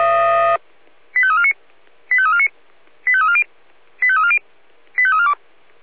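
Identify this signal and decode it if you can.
Five-tone selective call (ZVEI), heard as quick runs of five tones, tones of about 70 ms. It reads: 85349 85349 85340 85340 85342
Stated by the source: Five-tone ZVEI selective call 85275 received by SDR.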